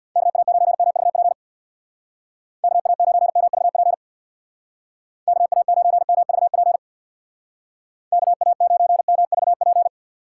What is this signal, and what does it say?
Morse code, heard as a keyed tone, about 60 wpm, 700 Hz. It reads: KA9MVP KA9MVP KA9MVP KA9MVP
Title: Speed Racing - USA 2x3 Call Signs 50wpm